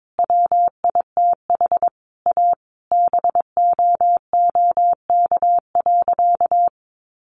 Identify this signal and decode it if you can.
Morse code, heard as a keyed tone, about 22 wpm, 700 Hz. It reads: WITH A BOOK.
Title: Easy Phrases 22wpm